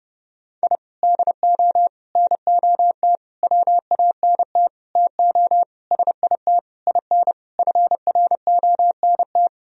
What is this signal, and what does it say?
Morse code, heard as a keyed tone, about 30 wpm, 700 Hz. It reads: I DO NOT WANT TO SIT IN FRONT
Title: Sentences from Top 1000 Words - Review Part 2 - 30wpm